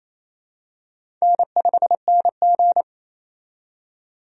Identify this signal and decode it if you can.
Morse code, heard as a keyed tone, about 27 wpm, 700 Hz. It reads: N5NG
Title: USA Call Signs - 4 Characters - 28wpm